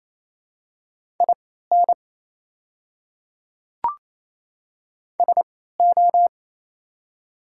Morse code, heard as a keyed tone, about 28 wpm, spaced as for 16 wpm, 700 Hz. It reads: IN SO
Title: 3x Character Spacing - Top 100 Words - 28wpm